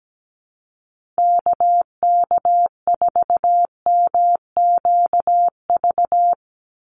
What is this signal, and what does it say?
Morse code, heard as a keyed tone, about 17 wpm, 700 Hz. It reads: KK4MQV
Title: USA Call Signs 17wpm